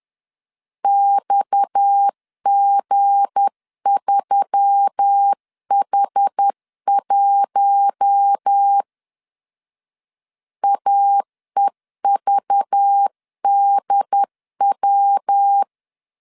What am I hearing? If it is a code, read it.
Morse code, heard as a keyed tone, about 10 wpm, 800 Hz. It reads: XG3H1 AEVDW
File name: LEZIONI / 7 GF9Z / 7 G 10wpm 2501